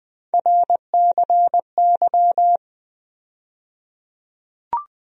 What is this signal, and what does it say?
Morse code, heard as a keyed tone, about 20 wpm, 700 Hz. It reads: RCY
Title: USA Call Sign Postfixes - 3 Letters 20wpm